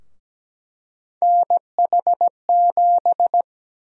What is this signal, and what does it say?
Morse code, heard as a keyed tone, about 17 wpm, 700 Hz. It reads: NH7
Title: USA Call Sign Prefixes - Repeated 2x - 17wpm